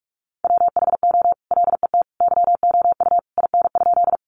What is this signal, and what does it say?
Morse code, heard as a keyed tone, about 44 wpm, 700 Hz. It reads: WHO LET YOU IN?
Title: Sentences from Top 300 Words 45wpm